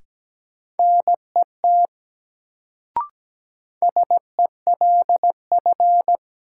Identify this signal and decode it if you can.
Morse code, heard as a keyed tone, about 17 wpm, 700 Hz. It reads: NET SELF
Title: Sentences from Top 1400 Words - New Word Drill Alt - 17wpm